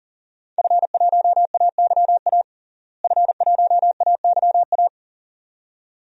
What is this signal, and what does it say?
Morse code, heard as a keyed tone, about 40 wpm, 700 Hz. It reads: F1AYA F1AYA
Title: DXCC-100 Call Signs - 5 Characters - Repeated 2x - 40wpm